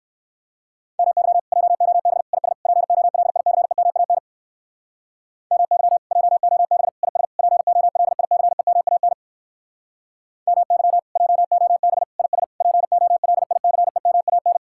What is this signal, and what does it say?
Morse code, heard as a keyed tone, about 66 wpm, 700 Hz. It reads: MY JOB IS POLICEMAN MY JOB IS POLICEMAN MY JOB IS POLICEMAN